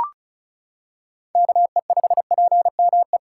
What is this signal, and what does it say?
Morse code, heard as a keyed tone, about 34 wpm, 690 Hz. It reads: KE5PME
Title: USA Callsign DE Callsign 35wpm